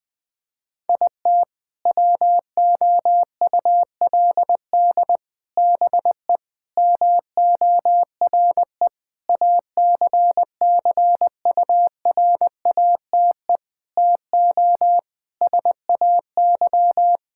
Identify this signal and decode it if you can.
Morse code, heard as a keyed tone, about 20 wpm, 700 Hz. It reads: IT WOULD BE MORE ACCURATE TO SAY
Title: Sentences with 8-Word Phrases - Spoken Beginning - 20wpm